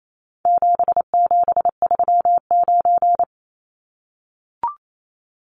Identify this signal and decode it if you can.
Morse code, heard as a keyed tone, about 28 wpm, 700 Hz. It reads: 7739
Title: Numbers - 4-Digit 28wpm